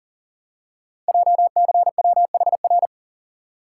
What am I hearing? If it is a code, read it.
Morse code, heard as a keyed tone, about 42 wpm, 700 Hz. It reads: JCWHR
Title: Code Groups - 5 Letters - 40wpm